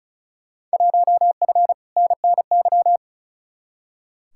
Morse code, heard as a keyed tone, about 36 wpm, 700 Hz. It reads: 1F NNY
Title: Field Day Exchanges - 35wpm